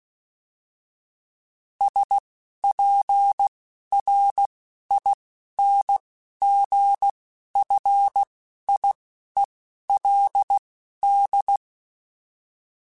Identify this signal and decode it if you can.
Morse code, heard as a keyed tone, about 16 wpm, spaced as for 11 wpm, 800 Hz. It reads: SPRINGFIELD